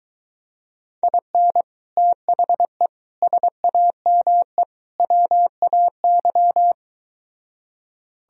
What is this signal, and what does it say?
Morse code, heard as a keyed tone, about 23 wpm, 700 Hz. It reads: IN THE SAME WAY